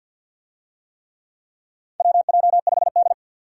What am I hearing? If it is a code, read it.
Morse code, heard as a keyed tone, about 53 wpm, 700 Hz. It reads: WJ5D
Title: USA Call Signs - 4 Characters - 50wpm